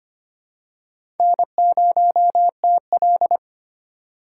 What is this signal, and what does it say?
Morse code, heard as a keyed tone, about 25 wpm, 700 Hz. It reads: N0TL
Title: USA Call Signs - 4 Characters - 2x Character Spacing - 25wpm